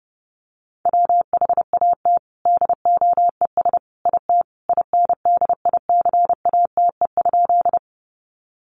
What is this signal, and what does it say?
Morse code, heard as a keyed tone, about 30 wpm, 700 Hz. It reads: WHAT DOES IT INDICATE?
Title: Sentences from Top 1300 Words 30wpm